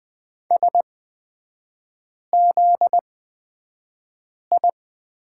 Morse code, heard as a keyed tone, about 20 wpm, spaced as for 5 wpm, 700 Hz. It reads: SZI
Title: 250 Lesson 40e - BK - Three Characters 20wpm